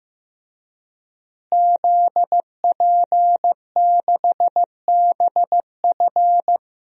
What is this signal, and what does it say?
Morse code, heard as a keyed tone, about 15 wpm, 700 Hz. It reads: ZP6BF
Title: DXCC-100 Call Signs - 5 Characters - Repeated 3x - 15wpm